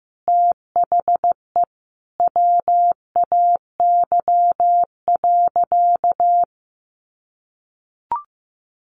Morse code, heard as a keyed tone, about 15 wpm, 700 Hz. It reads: THE WAY.
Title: Sentences from Top 1400 Words 15wpm